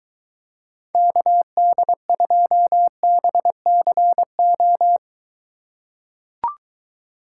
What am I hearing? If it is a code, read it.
Morse code, heard as a keyed tone, about 23 wpm, 700 Hz. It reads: KD2BCO